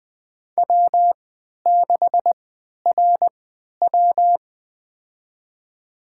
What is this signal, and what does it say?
Morse code, heard as a keyed tone, about 20 wpm, spaced as for 11 wpm, 700 Hz. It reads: W6RW
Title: USA Call Signs - 4 Characters - 3x Character Spacing - 20wpm